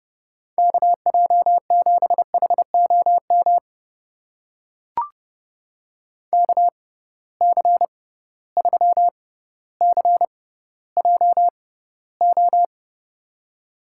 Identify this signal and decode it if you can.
Morse code, heard as a keyed tone, about 30 wpm, spaced as for 15 wpm, 700 Hz. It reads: KJ7HOM KC3CJO
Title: USA Call Signs - 6 Characters - 6x Character Spacing - 30wpm